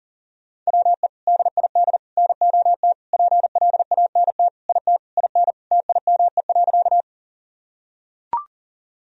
Morse code, heard as a keyed tone, about 40 wpm, 700 Hz. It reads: WE DID NOT PLANT IT IN TIME.
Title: Sentences from Top 1000 Words - Review Part 1 - 40wpm